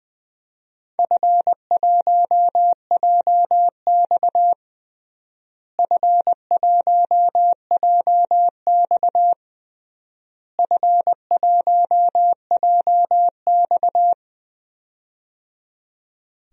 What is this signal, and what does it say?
Morse code, heard as a keyed tone, about 20 wpm, 700 Hz. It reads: F1JX F1JX F1JX
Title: DXCC-100 Call Signs - 4 Characters - Repeated 3x - 20wpm